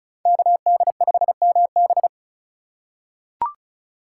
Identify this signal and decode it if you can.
Morse code, heard as a keyed tone, about 36 wpm, 690 Hz. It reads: KD5MB
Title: USA Call Signs - 5 Characters - 3x Character Spacing - 35wpm